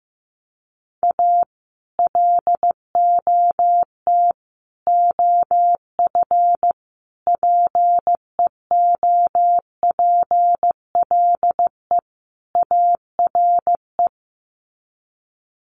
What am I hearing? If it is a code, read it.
Morse code, heard as a keyed tone, about 15 wpm, 700 Hz. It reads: A LOT OF PEOPLE ARE
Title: Sets of 5 Words 15wpm